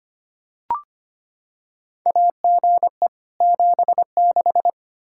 Morse code, heard as a keyed tone, about 25 wpm, 700 Hz. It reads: AGE 76
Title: K1USN SST QSO Practice 25wpm